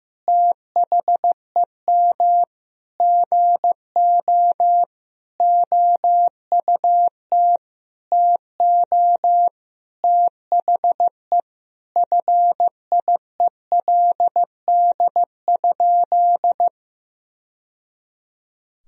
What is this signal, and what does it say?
Morse code, heard as a keyed tone, about 15 wpm, 700 Hz. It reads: THEM GO OUT TO THE FIELD?
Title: Sentences from Top 1000 Words - Review Part 2 - 15wpm